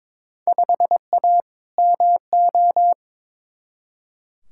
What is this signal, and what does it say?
Morse code, heard as a keyed tone, about 22 wpm, 700 Hz. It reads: 5A MO
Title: Field Day Exchanges Encore - 22wpm